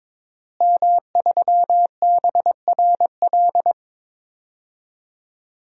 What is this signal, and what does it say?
Morse code, heard as a keyed tone, about 22 wpm, 700 Hz. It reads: M3BRL
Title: DXCC-100 Call Signs - 5 Characters - Repeated 3x - 22wpm